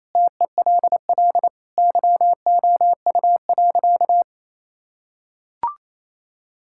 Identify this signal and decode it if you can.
Morse code, heard as a keyed tone, about 28 wpm, 700 Hz. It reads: TELL YOU.
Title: Sentences from Top 1000 Words - Review Part 5 - 28wpm